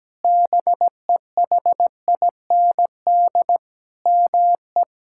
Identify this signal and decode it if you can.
Morse code, heard as a keyed tone, about 17 wpm, 700 Hz. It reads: BEHIND ME
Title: Sentences from Top 1600 Words 17wpm